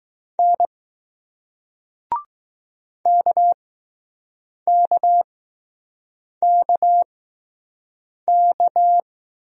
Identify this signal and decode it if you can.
Morse code, heard as a keyed tone, about 20 wpm, spaced as for 6 wpm, 700 Hz. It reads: N KKKK